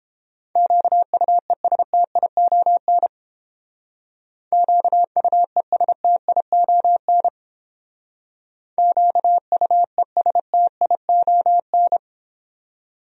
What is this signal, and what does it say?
Morse code, heard as a keyed tone, about 30 wpm, 700 Hz. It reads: QUESTION QUESTION QUESTION